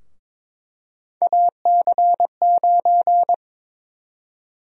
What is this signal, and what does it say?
Morse code, heard as a keyed tone, about 22 wpm, 700 Hz. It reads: AC9
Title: USA Call Sign Prefixes - 22wpm